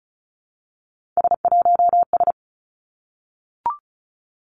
Morse code, heard as a keyed tone, about 34 wpm, 700 Hz. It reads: S1S
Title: Three Challenging Letter-Number Combinations 35wpm